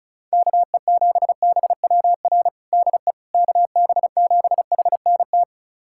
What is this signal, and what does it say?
Morse code, heard as a keyed tone, about 34 wpm, 700 Hz. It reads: KE7BWR DE KB7HNT
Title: USA Callsign DE Callsign - 6 Characters - 35wpm